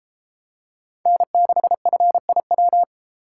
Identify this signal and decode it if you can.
Morse code, heard as a keyed tone, about 32 wpm, 700 Hz. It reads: N6FIW